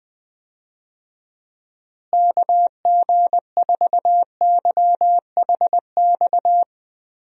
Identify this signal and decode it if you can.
Morse code, heard as a keyed tone, about 20 wpm, 700 Hz. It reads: KG4YHX
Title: USA Call Signs - 6 Characters - Repeated 3x - 20wpm